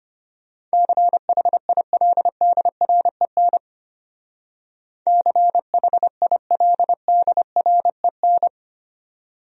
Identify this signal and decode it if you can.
Morse code, heard as a keyed tone, about 32 wpm, 700 Hz. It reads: CHILDREN CHILDREN